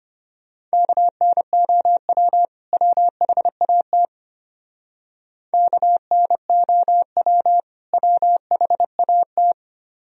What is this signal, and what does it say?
Morse code, heard as a keyed tone, about 31 wpm, 700 Hz. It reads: KNOW WHAT KNOW WHAT